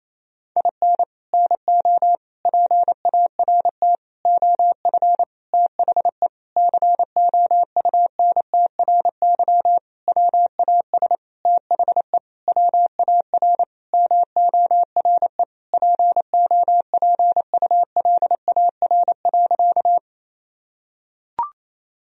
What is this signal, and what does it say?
Morse code, heard as a keyed tone, about 27 wpm, 700 Hz. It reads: IN NO PART OF THE COUNTRY WAS THE WAR MORE POPULAR.
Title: Sentences from Top 1000 Words - Review Part 3 - 28wpm